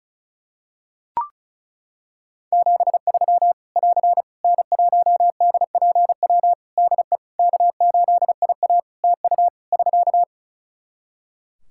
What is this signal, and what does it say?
Morse code, heard as a keyed tone, about 34 wpm, 700 Hz. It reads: 73 <AR> N1DPW DE K8IA TU <SK>
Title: K1USN SST QSO Practice 35wpm